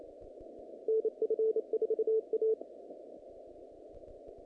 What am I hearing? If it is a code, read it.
Morse code, heard as a keyed tone, about 27 wpm, 440 Hz. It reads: NF4A
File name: contests / naqp-cw-k5tr-op-k5pi- -2200